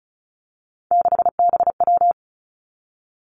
Morse code, heard as a keyed tone, about 36 wpm, 700 Hz. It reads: 6BW
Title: Three Challenging Letter-Number Combinations 35wpm